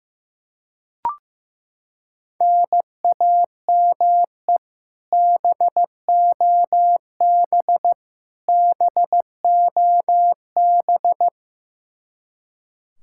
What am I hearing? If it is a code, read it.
Morse code, heard as a keyed tone, about 16 wpm, 700 Hz. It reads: NAME BOB BOB
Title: K1USN SST QSO Practice 15wpm